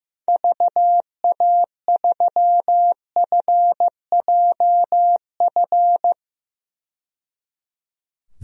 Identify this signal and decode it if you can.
Morse code, heard as a keyed tone, about 15 wpm, 700 Hz. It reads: VA3FJF
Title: DXCC-100 Call Signs - 6 Characters - 15wpm